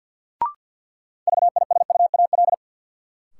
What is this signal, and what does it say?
Morse code, heard as a keyed tone, about 47 wpm, 700 Hz. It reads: VISUAL
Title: Sets of 5 Words Encore - New Word Drill - 50wpm